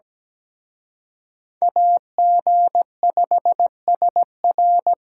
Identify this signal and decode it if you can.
Morse code, heard as a keyed tone, about 20 wpm, 700 Hz. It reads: AG5SR
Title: Speed Racing - USA Call Signs 17wpm